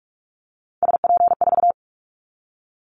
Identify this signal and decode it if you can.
Morse code, heard as a keyed tone, about 46 wpm, 700 Hz. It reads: SP4
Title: Callsign Prefixes for Top 25 DXCC - 45wpm